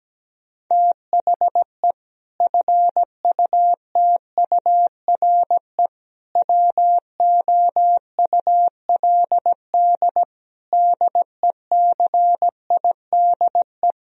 Morse code, heard as a keyed tone, about 17 wpm, 700 Hz. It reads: THE FUTURE WOULD DECIDE
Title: Sentences from Top 1000 Words - Review Part 3 - 17wpm